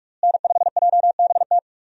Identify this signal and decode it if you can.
Morse code, heard as a keyed tone, about 41 wpm, 700 Hz. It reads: N5JBT
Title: USA Call Signs - 5 Characters - Repeated 3x - 45wpm